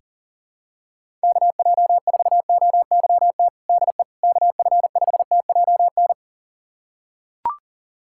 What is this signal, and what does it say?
Morse code, heard as a keyed tone, about 37 wpm, 700 Hz. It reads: KJ4OYT DE KF5TJN
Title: USA Callsign DE Callsign - 6 Characters - 40wpm